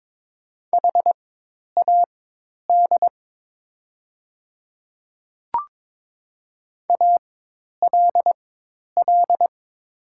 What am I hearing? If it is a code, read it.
Morse code, heard as a keyed tone, about 22 wpm, spaced as for 10 wpm, 700 Hz. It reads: HAD ALL
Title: 4x Character Spacing - Top 100 Words - 22wpm